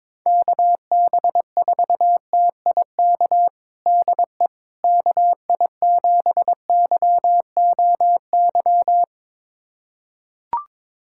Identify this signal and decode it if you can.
Morse code, heard as a keyed tone, about 22 wpm, 700 Hz. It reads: KB4TIK DE KI7YOY
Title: USA Callsign DE Callsign - 6 Characters - 22wpm